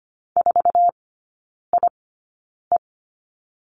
Morse code, heard as a keyed tone, about 25 wpm, spaced as for 9 wpm, 700 Hz. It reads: 4IE
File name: Lesson 17e - Letter u - Three Characters 25wpm